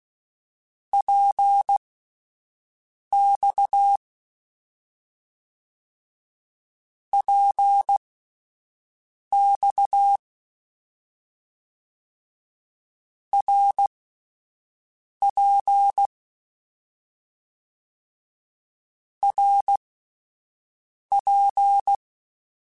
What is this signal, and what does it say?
Morse code, heard as a keyed tone, about 16 wpm, spaced as for 5 wpm, 800 Hz. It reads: PX PX RP RP
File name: Short Runs / Short Runs-mp3s / Balanced @10